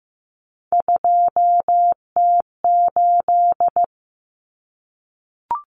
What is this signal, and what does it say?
Morse code, heard as a keyed tone, about 15 wpm, 700 Hz. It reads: 2T8
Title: Common Cut Numbers - 3-Digit 15wpm